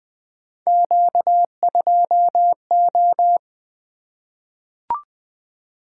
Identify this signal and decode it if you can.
Morse code, heard as a keyed tone, about 20 wpm, 700 Hz. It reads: Q2O
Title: Three Letter-Number Combinations - 20wpm